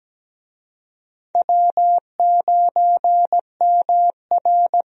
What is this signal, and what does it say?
Morse code, heard as a keyed tone, about 17 wpm, 700 Hz. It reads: W9MR
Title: USA Call Signs - 4 Characters - Repeated 3x - 17wpm